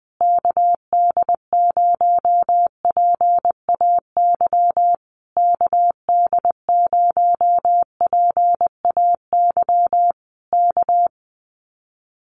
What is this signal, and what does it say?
Morse code, heard as a keyed tone, about 19 wpm, 700 Hz. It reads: KD0PAY KD0PAY K
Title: Common QSO Phrases Encore 20wpm